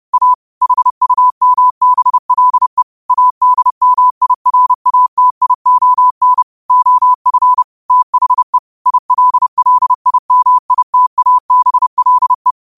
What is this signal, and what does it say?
Morse code, heard as a keyed tone, about 30 wpm, 1000 Hz. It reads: A HUMBLE ADMIRATION OF THE ILLIMITABLE
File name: qotdfiles / 30wpm